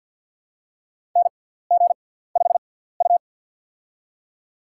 Morse code, heard as a keyed tone, about 53 wpm, spaced as for 17 wpm, 700 Hz. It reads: NG5U
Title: USA Call Signs - 4 Characters - 6x Character Spacing - 50wpm